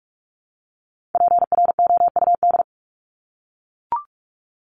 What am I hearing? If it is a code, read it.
Morse code, heard as a keyed tone, about 44 wpm, 700 Hz. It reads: PROUD
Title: Top 1501-1750 Words 45wpm